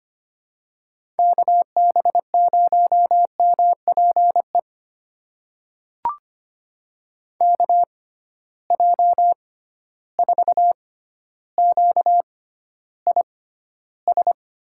Morse code, heard as a keyed tone, about 25 wpm, spaced as for 13 wpm, 700 Hz. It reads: KB0MPE KJ4QIS
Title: USA Call Signs - 6 Characters - 6x Character Spacing - 25wpm